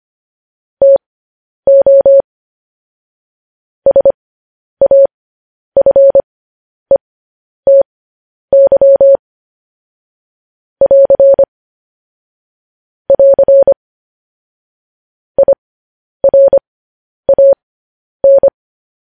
Morse code, heard as a keyed tone, about 25 wpm, spaced as for 10 wpm, 550 Hz. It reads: TO SAFETY <AR> <AR> IRAN
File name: podcast / News headlines-1x-25x10